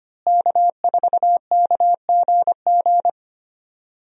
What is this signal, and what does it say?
Morse code, heard as a keyed tone, about 25 wpm, 700 Hz. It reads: K4KGG
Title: USA Call Signs - 5 Characters - Repeated 2x - 25wpm